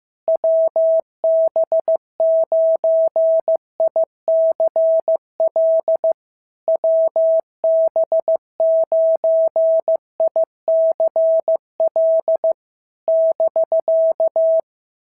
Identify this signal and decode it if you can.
Morse code, heard as a keyed tone, about 15 wpm, 650 Hz. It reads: WB9ICL WB9ICL <BK>
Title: SOTA QSO Practice 15wpm